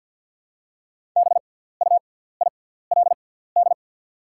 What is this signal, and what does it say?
Morse code, heard as a keyed tone, about 48 wpm, spaced as for 17 wpm, 700 Hz. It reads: BUILD
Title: 6x Character Spacing - Top 300 Words - 50wpm